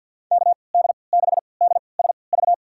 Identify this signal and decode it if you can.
Morse code, heard as a keyed tone, about 52 wpm, spaced as for 29 wpm, 700 Hz. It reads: KD6DSV
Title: USA Call Signs - 6 Characters - 3x Character Spacing - 50wpm